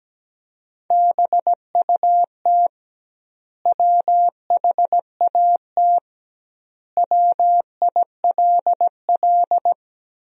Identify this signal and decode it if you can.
Morse code, heard as a keyed tone, about 17 wpm, 700 Hz. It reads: BUT WHAT WILL
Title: 2x Word-Spacing - Sentences from Top 100 Words - 17wpm